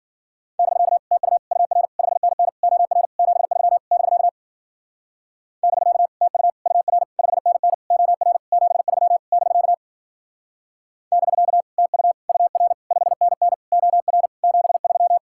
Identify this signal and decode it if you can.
Morse code, heard as a keyed tone, about 52 wpm, 700 Hz. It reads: <BK> TU UR 5NN OR 73 <BK> <BK> TU UR 5NN OR 73 <BK> <BK> TU UR 5NN OR 73